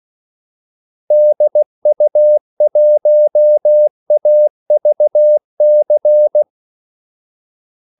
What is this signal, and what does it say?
Morse code, heard as a keyed tone, about 16 wpm, 600 Hz. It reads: DU1AVC
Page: Kaldesignaler 16 wpm